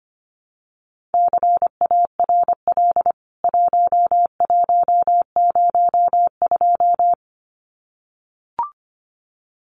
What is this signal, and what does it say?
Morse code, heard as a keyed tone, about 25 wpm, 700 Hz. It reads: CARL 1102
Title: CWT Exchange 25wpm